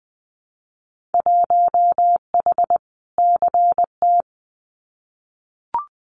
Morse code, heard as a keyed tone, about 20 wpm, 700 Hz. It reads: 1H CT
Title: Winter Field Day Exchanges 20wpm